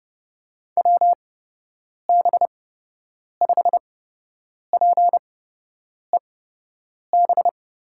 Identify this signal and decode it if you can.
Morse code, heard as a keyed tone, about 30 wpm, spaced as for 8 wpm, 700 Hz. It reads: WB5PEB
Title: USA Call Signs - 6 Characters - 8x Character Spacing - 30wpm